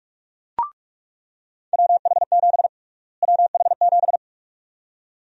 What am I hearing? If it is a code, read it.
Morse code, heard as a keyed tone, about 46 wpm, 700 Hz. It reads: WH7 WH7
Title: USA Call Sign Prefixes - Repeated 2x - 45wpm